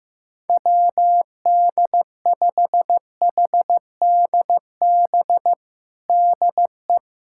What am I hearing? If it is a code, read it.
Morse code, heard as a keyed tone, about 15 wpm, 700 Hz. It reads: WD5HDB DE
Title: USA Callsign DE Callsign - 6 Characters - 15wpm